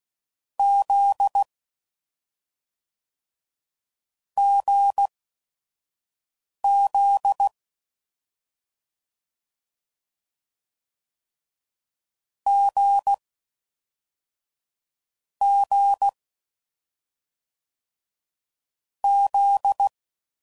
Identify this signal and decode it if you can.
Morse code, heard as a keyed tone, about 16 wpm, spaced as for 5 wpm, 800 Hz. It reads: Z GZ G G Z
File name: Combinations / Difficult Combos / awj1 ndb6 gz7